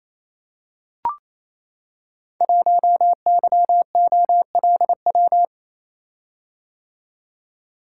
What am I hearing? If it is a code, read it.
Morse code, heard as a keyed tone, about 29 wpm, 700 Hz. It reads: 1YOLW
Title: Code Groups - 5 Letter:Numbers - 28wpm